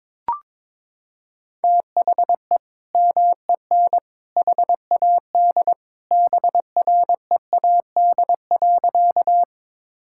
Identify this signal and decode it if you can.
Morse code, heard as a keyed tone, about 22 wpm, 700 Hz. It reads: THE MEN HAD BREAD.
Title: Sentences from Top 1600 Words 22wpm